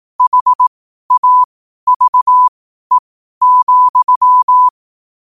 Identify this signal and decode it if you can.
Morse code, heard as a keyed tone, about 18 wpm, spaced as for 12 wpm, 1000 Hz. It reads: HAVE,
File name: qotdfiles / 12wpm